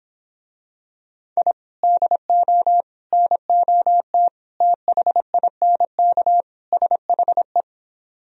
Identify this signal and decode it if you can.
Morse code, heard as a keyed tone, about 26 wpm, 700 Hz. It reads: I DO NOT THINK SHE